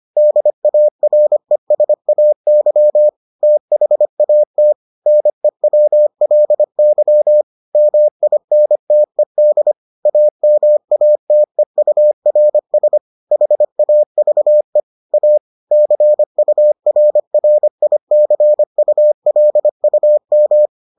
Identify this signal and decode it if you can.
Morse code, heard as a keyed tone, about 25 wpm, 600 Hz. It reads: DARESAY THAT NEWLY MINTED AMATEURS HAVE A CURRICULUM